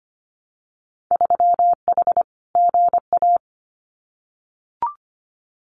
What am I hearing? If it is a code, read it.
Morse code, heard as a keyed tone, about 25 wpm, 700 Hz. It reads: 3H GA
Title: Winter Field Day Exchanges Encore 25wpm